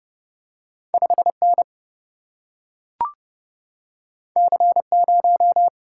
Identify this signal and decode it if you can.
Morse code, heard as a keyed tone, about 30 wpm, 700 Hz. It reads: 5 N C 0
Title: Single-Letter Single-Number 30wpm